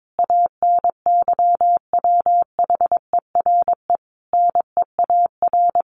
Morse code, heard as a keyed tone, about 22 wpm, 700 Hz. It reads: ANYWHERE NEAR
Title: Sentences from Top 1400 Words 22wpm